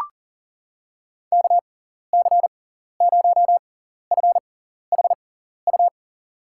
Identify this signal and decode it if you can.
Morse code, heard as a keyed tone, about 37 wpm, spaced as for 14 wpm, 700 Hz. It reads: KC0FHU
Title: USA Call Signs - 6 Characters - 6x Character Spacing - 40wpm